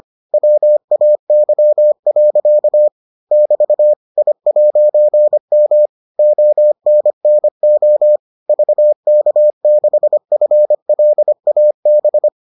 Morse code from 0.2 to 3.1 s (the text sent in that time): WAY.